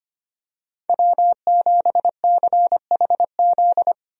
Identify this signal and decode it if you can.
Morse code, heard as a keyed tone, about 25 wpm, 700 Hz. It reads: W7CHZ
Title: USA Call Signs - 5 Characters - 25wpm